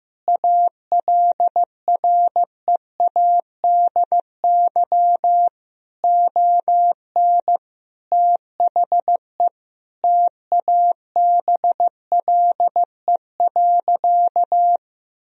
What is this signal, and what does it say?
Morse code, heard as a keyed tone, about 16 wpm, 700 Hz. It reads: ALREADY ON THE TABLE.
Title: Sentences from Top 1000 Words - Review Part 3 - 15wpm